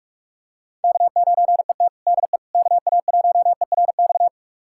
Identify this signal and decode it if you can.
Morse code, heard as a keyed tone, about 46 wpm, 700 Hz. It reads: K9ET DE KA1ERX
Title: USA Callsign DE Callsign 45wpm